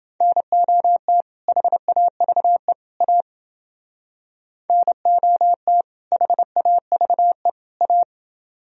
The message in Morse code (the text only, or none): NOT HAVE A NOT HAVE A